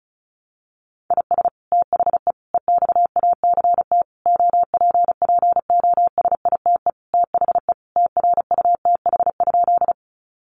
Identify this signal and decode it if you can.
Morse code, heard as a keyed tone, about 34 wpm, 700 Hz. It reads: IS THE EXACT OPPOSITE THE TRUTH?
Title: Sentences from Top 1300 Words 35wpm